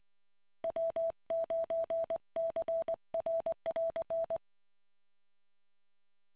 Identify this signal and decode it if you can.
Morse code, heard as a keyed tone, about 20 wpm, 660 Hz. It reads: W9CR<AR>